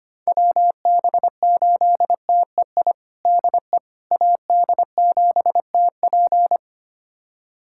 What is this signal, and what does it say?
Morse code, heard as a keyed tone, about 25 wpm, 700 Hz. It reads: WB8TEI DE AD7TP
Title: USA Callsign DE Callsign 25wpm